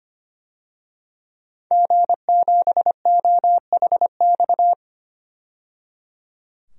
Morse code, heard as a keyed tone, about 25 wpm, 700 Hz. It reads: G7OHX
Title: DXCC-100 Call Signs - 5 Characters - 25wpm